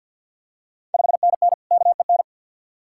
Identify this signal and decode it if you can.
Morse code, heard as a keyed tone, about 50 wpm, 700 Hz. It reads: 5NN KEN
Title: Ontario QSO Party Exchange 50wpm